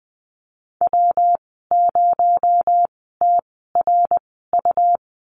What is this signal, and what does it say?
Morse code, heard as a keyed tone, about 20 wpm, spaced as for 14 wpm, 700 Hz. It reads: W0TRU
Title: 231 Lesson 37g - Letter x - Call Signs 20wpm